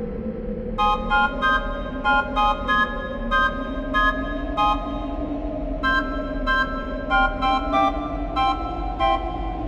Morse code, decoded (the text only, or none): none